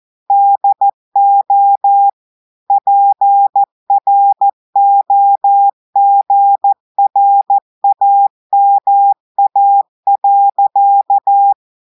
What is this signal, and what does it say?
Morse code, heard as a keyed tone, about 14 wpm, 800 Hz. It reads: DO PROGRAMA.
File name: audios / portugais / portugais texte en clair 2 vitesse 14 mots minutes